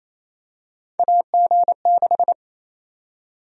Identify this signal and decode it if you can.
Morse code, heard as a keyed tone, about 28 wpm, 700 Hz. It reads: AG6
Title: USA Call Sign Prefixes - 28wpm